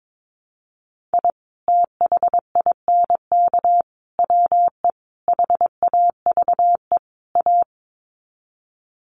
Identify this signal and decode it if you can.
Morse code, heard as a keyed tone, about 22 wpm, 700 Hz. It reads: I THINK WE HAVE A
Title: Sets of 5 Words Encore 22wpm